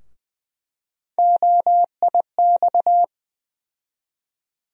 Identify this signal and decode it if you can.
Morse code, heard as a keyed tone, about 20 wpm, 700 Hz. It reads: OIX
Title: USA Call Sign Postfixes - 3 Letters 20wpm